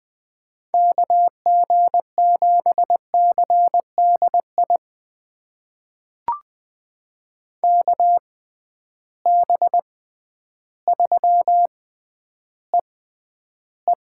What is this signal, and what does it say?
Morse code, heard as a keyed tone, about 20 wpm, spaced as for 11 wpm, 700 Hz. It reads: KG7CDI KB3EE
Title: USA Call Signs - 6 Characters - 6x Character Spacing - 20wpm